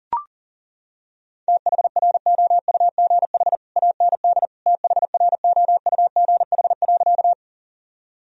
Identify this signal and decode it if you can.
Morse code, heard as a keyed tone, about 37 wpm, 700 Hz. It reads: THROUGH AND THROUGH.
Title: Sentences from Top 1000 Words - Review Part 1 - 40wpm